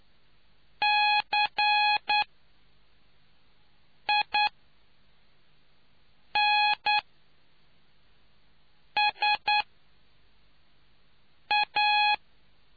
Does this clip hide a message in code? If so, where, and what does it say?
Yes: Morse code from 0.8 to 12.1 s, heard as a keyed tone, about 10 wpm, spaced as for 4 wpm, 840 Hz. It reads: CINSA